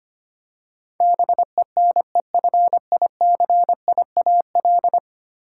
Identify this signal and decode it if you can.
Morse code, heard as a keyed tone, about 25 wpm, 700 Hz. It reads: BENEFICIAL
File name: 10 Letter Words - First 3 Letters Given - 25wpm